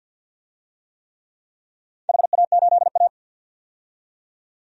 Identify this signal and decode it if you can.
Morse code, heard as a keyed tone, about 53 wpm, 700 Hz. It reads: HA8A
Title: DXCC-100 Call Signs - 4 Characters - 50wpm